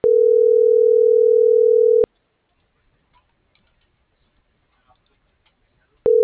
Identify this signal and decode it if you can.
Ringback tone (North America), 440 + 480 Hz.